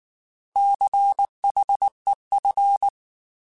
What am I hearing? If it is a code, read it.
Morse code, heard as a keyed tone, about 19 wpm, 800 Hz. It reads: CHEF